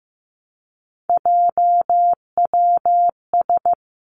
Morse code, heard as a keyed tone, about 15 wpm, 700 Hz. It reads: JWS
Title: Three Challenging Letter-Number Combinations 15wpm